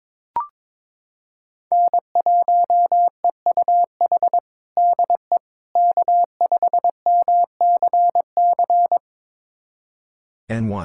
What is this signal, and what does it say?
Morse code, heard as a keyed tone, about 22 wpm, 700 Hz. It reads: N1EUH DE K5MCC
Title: USA Callsign DE Callsign - 5 Characters - 22wpm